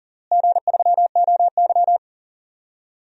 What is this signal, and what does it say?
Morse code, heard as a keyed tone, about 40 wpm, 700 Hz. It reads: G3OY
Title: DXCC-100 Call Signs - 4 Characters - Repeated 3x - 40wpm